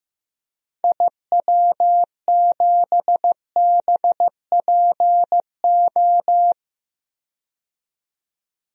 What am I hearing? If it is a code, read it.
Morse code, heard as a keyed tone, about 15 wpm, 700 Hz. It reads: IW7BPO
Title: DXCC-100 Call Signs - 6 Characters - 15wpm